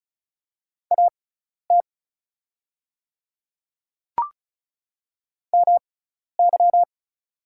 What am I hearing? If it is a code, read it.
Morse code, heard as a keyed tone, about 34 wpm, spaced as for 12 wpm, 700 Hz. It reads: AT MY
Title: 6x Character Spacing - Top 100 Words - 35wpm